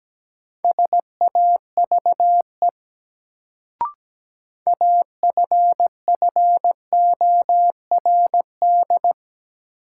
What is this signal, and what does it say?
Morse code, heard as a keyed tone, about 17 wpm, 700 Hz. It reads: SAVE AFFORD
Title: Sentences from Top 1100 Words - New Word Drill - 17wpm